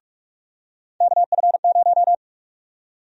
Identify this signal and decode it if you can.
Morse code, heard as a keyed tone, about 42 wpm, 700 Hz. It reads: KF0
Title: USA Call Sign Prefixes - 45wpm